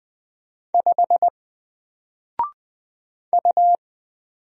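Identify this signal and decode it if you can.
Morse code, heard as a keyed tone, about 20 wpm, 700 Hz. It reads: 5 U
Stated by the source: Single letter and number Morse drill at 20 wpm, from N4LCD.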